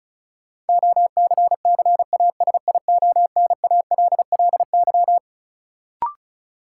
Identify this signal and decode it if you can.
Morse code, heard as a keyed tone, about 36 wpm, 700 Hz. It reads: OCCASIONALLY
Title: Sentences from Top 1300 Words - New Word Drill Alt - 35wpm